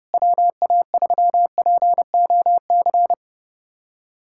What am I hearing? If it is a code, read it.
Morse code, heard as a keyed tone, about 30 wpm, 700 Hz. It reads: WA3POC
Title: USA Callsign DE Callsign 30wpm